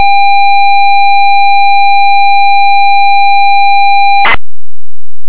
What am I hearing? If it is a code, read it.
ZX Spectrum tape loading.
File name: Standard tape loader - example